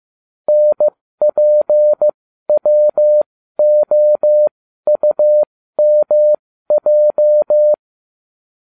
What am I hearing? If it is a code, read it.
Morse code, heard as a keyed tone, about 15 wpm, spaced as for 12 wpm, 600 Hz. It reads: NPWOUMJ